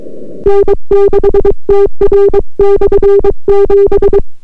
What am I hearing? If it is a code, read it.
Morse code, heard as a keyed tone, about 22 wpm, 400 Hz. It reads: N6TR/7